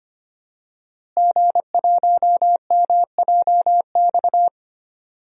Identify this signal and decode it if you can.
Morse code, heard as a keyed tone, about 25 wpm, 700 Hz. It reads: G1MJX
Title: DXCC-100 Call Signs - 5 Characters - Repeated 3x - 25wpm